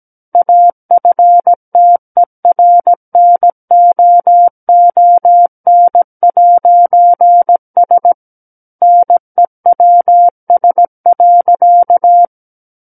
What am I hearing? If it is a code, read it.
Morse code, heard as a keyed tone, about 17 wpm, 700 Hz. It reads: AFTERNOON'S NEWS.
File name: Book One / 08-Chapter 8 - 17 WPM